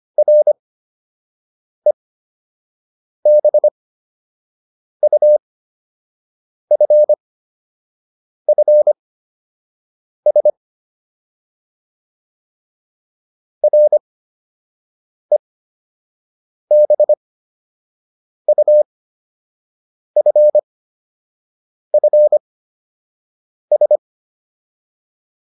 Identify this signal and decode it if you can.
Morse code, heard as a keyed tone, about 25 wpm, spaced as for 6 wpm, 600 Hz. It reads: REBUFFS REBUFFS